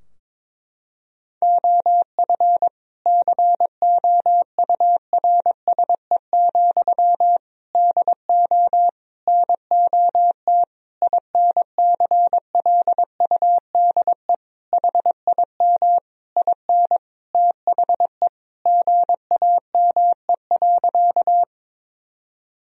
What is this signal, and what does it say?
Morse code, heard as a keyed tone, about 21 wpm, 700 Hz. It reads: OF COURSE, DO NOT INCLUDE HIM IN THE GAME.
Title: Sentences from Top 1000 Words - Review Part 3 - 22wpm